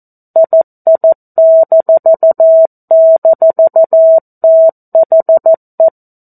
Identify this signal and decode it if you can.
Morse code, heard as a keyed tone, about 14 wpm, 650 Hz. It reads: II--THE
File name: 02-Chapter-02 14 WPM